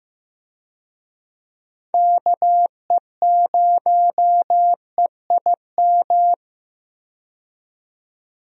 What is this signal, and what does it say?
Morse code, heard as a keyed tone, about 15 wpm, 700 Hz. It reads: KE0EIM
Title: USA Call Signs - 6 Characters - 15wpm